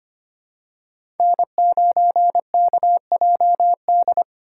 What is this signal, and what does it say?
Morse code, heard as a keyed tone, about 25 wpm, 700 Hz. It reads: N9KJD
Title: USA Call Signs - 5 Characters - 3x Character Spacing - 25wpm